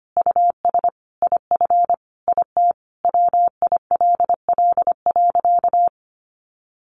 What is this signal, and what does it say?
Morse code, heard as a keyed tone, about 25 wpm, 700 Hz. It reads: US IF IT WILL.
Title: Sentences from Top 1500 Words 25wpm